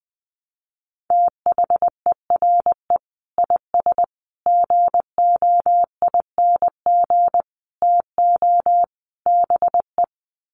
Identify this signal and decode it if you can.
Morse code, heard as a keyed tone, about 20 wpm, 700 Hz. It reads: THERE IS GOING TO BE
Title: Sets of 5 Words Encore 20wpm